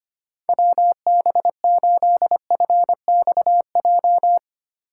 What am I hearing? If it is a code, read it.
Morse code, heard as a keyed tone, about 25 wpm, 700 Hz. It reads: WB8FXJ
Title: K1USN SST QSO Practice - 25wpm